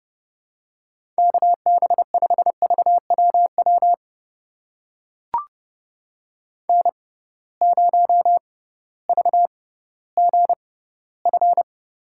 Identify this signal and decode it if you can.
Morse code, heard as a keyed tone, about 32 wpm, spaced as for 17 wpm, 700 Hz. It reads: KB5VWW N0VGF
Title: USA Call Signs - 6x Character Spacing - 30wpm